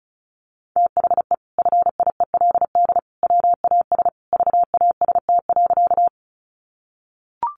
Morse code, heard as a keyed tone, about 34 wpm, 700 Hz. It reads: THE FIELD WAS VAST.
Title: Sentences from Top 1300 Words 35wpm